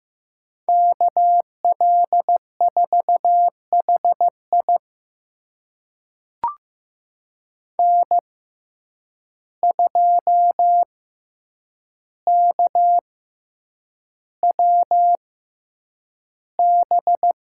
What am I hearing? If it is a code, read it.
Morse code, heard as a keyed tone, about 15 wpm, spaced as for 8 wpm, 700 Hz. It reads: KL4HI N2KWB
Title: USA Call Signs - 6x Character Spacing - 15wpm